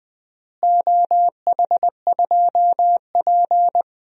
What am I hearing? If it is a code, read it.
Morse code, heard as a keyed tone, about 20 wpm, 700 Hz. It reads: OH2P
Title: DXCC-100 Call Signs - 4 Characters - Repeated 3x - 20wpm